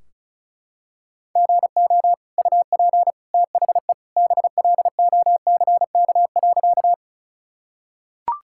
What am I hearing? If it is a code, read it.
Morse code, heard as a keyed tone, about 36 wpm, 700 Hz. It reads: GO UP THE BLOCK.
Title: Sentences from Top 1600 Words 35wpm